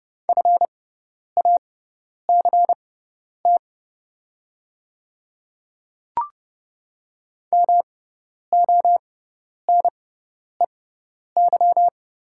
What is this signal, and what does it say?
Morse code, heard as a keyed tone, about 30 wpm, spaced as for 10 wpm, 690 Hz. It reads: FACT MONEY
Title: 6x Character Spacing - Top 500 Words - 30wpm